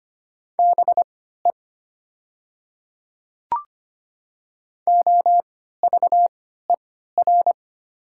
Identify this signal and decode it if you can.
Morse code, heard as a keyed tone, about 25 wpm, spaced as for 14 wpm, 700 Hz. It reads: BE OVER
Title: 3x Character Spacing - Top 100 Words - 25wpm